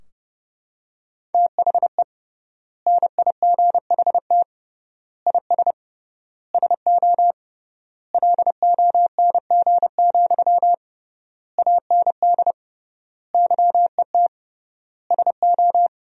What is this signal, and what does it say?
Morse code, heard as a keyed tone, about 32 wpm, 700 Hz. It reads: THE NIGHT IS SO LONG, AND YET SO
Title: 3x Word-Spacing - Sentences from Top 500 Words - 30wpm